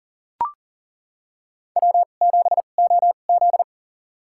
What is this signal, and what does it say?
Morse code, heard as a keyed tone, about 40 wpm, spaced as for 29 wpm, 700 Hz. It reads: W7OZ
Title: USA Call Signs - 2x Character Spacing - 40wpm